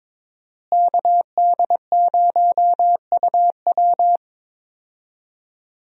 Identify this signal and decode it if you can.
Morse code, heard as a keyed tone, about 22 wpm, 700 Hz. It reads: KD0UW
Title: USA Call Signs - 5 Characters - Repeated 2x - 22wpm